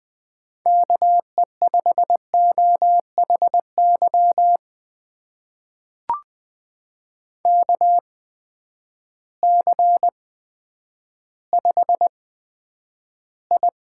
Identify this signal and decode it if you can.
Morse code, heard as a keyed tone, about 20 wpm, spaced as for 10 wpm, 700 Hz. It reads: KE5OHY KC5I
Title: USA Call Signs - 6 Characters - 8x Character Spacing - 20wpm